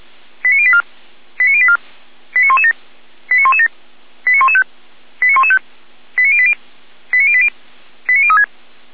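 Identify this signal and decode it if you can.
Five-tone selective call (ZVEI), heard as quick runs of five tones, tones of about 70 ms. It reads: 89084 89084 89117 89117 89115 89115 89080 89080 89036
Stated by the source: ZVEI five-tone call 89501 received by SDR, archived by Feuerwehr Orlamünde fire brigade.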